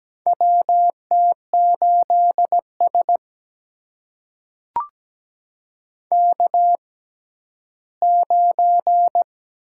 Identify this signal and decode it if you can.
Morse code, heard as a keyed tone, about 17 wpm, spaced as for 11 wpm, 700 Hz. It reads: WT8S K9
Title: USA Call Signs - 4 Characters - 6x Character Spacing - 17wpm